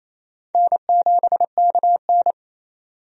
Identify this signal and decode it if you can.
Morse code, heard as a keyed tone, about 28 wpm, 700 Hz. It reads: N7KN
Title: USA Call Signs - 4 Characters - 28wpm